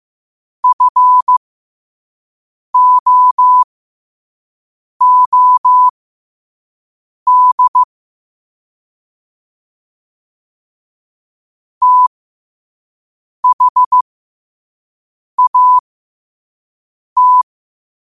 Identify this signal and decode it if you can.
Morse code, heard as a keyed tone, about 15 wpm, spaced as for 5 wpm, 1000 Hz. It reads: FOOD THAT
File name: qotdfiles / 5wpm